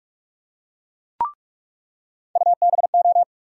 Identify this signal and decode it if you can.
Morse code, heard as a keyed tone, about 44 wpm, 700 Hz. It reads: UBO
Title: USA Call Sign Postfixes - 3 Letters 45wpm